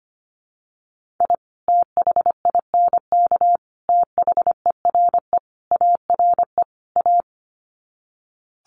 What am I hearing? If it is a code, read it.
Morse code, heard as a keyed tone, about 25 wpm, 700 Hz. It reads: I THINK THERE ARE A
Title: Sets of 5 Words Encore2 25wpm